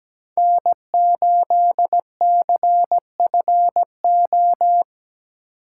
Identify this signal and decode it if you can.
Morse code, heard as a keyed tone, about 17 wpm, 700 Hz. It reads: N8CFO
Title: USA Call Signs - 5 Characters - 8x Character Spacing - 17wpm